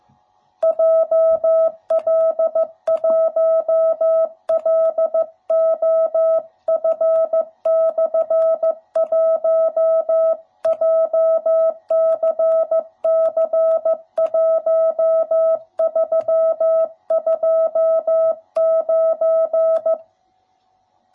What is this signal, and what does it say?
Morse code, heard as a keyed tone, about 14 wpm, 660 Hz. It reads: JL1LOF/1 JCC1329
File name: jh8cbh / 2021cw / cw07b